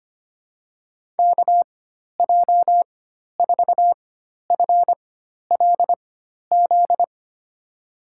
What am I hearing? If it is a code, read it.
Morse code, heard as a keyed tone, about 25 wpm, spaced as for 12 wpm, 700 Hz. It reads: KJ4FLZ
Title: USA Call Signs - 6 Characters - 4x Character Spacing - 25wpm